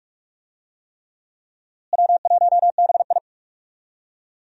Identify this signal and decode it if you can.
Morse code, heard as a keyed tone, about 42 wpm, 700 Hz. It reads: W1BI
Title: USA Call Signs - 4 Characters - Repeated 3x - 45wpm